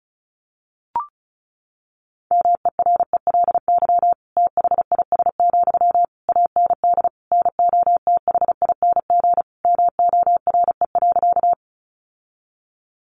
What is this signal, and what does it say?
Morse code, heard as a keyed tone, about 34 wpm, 700 Hz. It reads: MERELY THIS, AND NOTHING MORE.
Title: Sentences from Top 1400 Words 35wpm